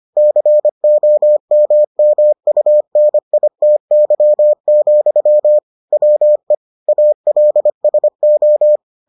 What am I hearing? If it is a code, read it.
Morse code, heard as a keyed tone, about 25 wpm, 600 Hz. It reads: COMMUNITY, WE ALSO